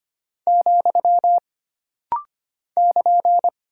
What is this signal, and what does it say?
Morse code, heard as a keyed tone, about 25 wpm, 700 Hz. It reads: , <KN>
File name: Single Punctuation-Prosign - Rapid-Fire - 25wpm